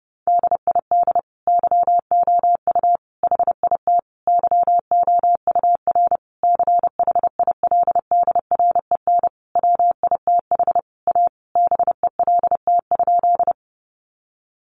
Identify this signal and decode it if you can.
Morse code, heard as a keyed tone, about 30 wpm, 700 Hz. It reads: DID YOU HIT YOUR CHILDREN WITH A BELT?
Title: Sentences from Top 1400 Words 30wpm